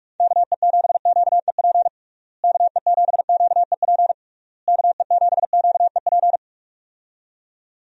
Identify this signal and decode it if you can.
Morse code, heard as a keyed tone, about 46 wpm, 700 Hz. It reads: KE7QEP KE7QEP KE7QEP
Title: USA Call Signs - 6 Characters - Repeated 3x - 45wpm